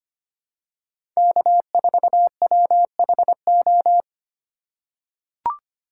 Morse code, heard as a keyed tone, about 25 wpm, 700 Hz. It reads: K4WHO
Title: USA Call Signs - 4x Character Spacing - 25wpm